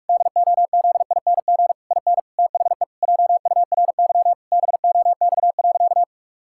Morse code, heard as a keyed tone, about 41 wpm, 700 Hz. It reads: DOZING IN THE JURY BOX.
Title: Pangram Challenge v2 - 45wpm